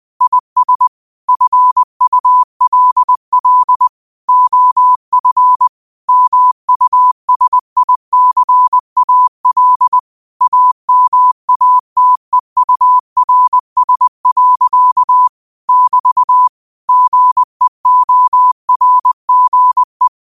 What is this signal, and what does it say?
Morse code, heard as a keyed tone, about 20 wpm, 1000 Hz. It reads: IS FULL OF MUSICAL AMATEURS. <BT> GEORGE
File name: qotdfiles / 20wpm